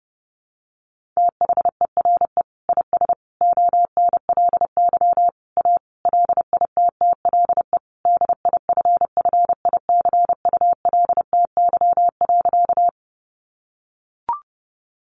Morse code, heard as a keyed tone, about 30 wpm, 700 Hz. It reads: THERE IS ONLY A LITTLE DIFFICULTY.
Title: Sentences from Top 1500 Words 30wpm